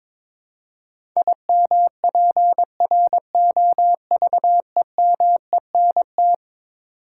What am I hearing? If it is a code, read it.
Morse code, heard as a keyed tone, about 22 wpm, 700 Hz. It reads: IMPROVEMENT
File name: practice / 22wpm / Eleven Letter Words - 22wpm